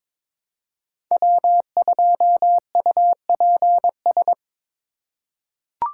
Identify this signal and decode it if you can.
Morse code, heard as a keyed tone, about 22 wpm, 700 Hz. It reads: W2UPS
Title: USA Call Signs - 5 Characters - 3x Character Spacing - 22wpm